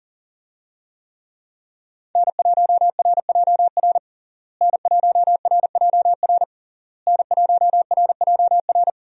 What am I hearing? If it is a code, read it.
Morse code, heard as a keyed tone, about 40 wpm, 700 Hz. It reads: N1RJR N1RJR N1RJR
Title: USA Call Signs - 5 Characters - Repeated 3x - 40wpm